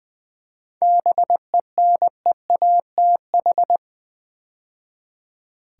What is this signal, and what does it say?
Morse code, heard as a keyed tone, about 20 wpm, 700 Hz. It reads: BENEATH
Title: 7 Letter Words - First 3 Letters Given - 20wpm